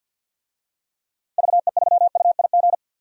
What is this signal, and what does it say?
Morse code, heard as a keyed tone, about 52 wpm, 700 Hz. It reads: VE3UIG
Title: DXCC-100 Call Signs - 6 Characters - 50wpm